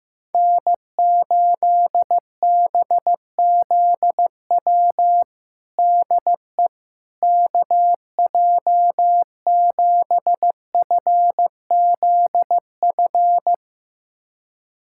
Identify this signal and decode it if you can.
Morse code, heard as a keyed tone, about 15 wpm, 700 Hz. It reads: N8BZW DE KJ7FZF
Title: USA Callsign DE Callsign 15wpm